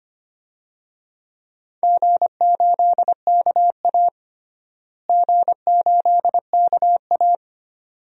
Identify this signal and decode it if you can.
Morse code, heard as a keyed tone, about 25 wpm, 700 Hz. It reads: G8KA G8KA
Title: DXCC-100 Call Signs - 4 Characters - Repeated 2x - 25wpm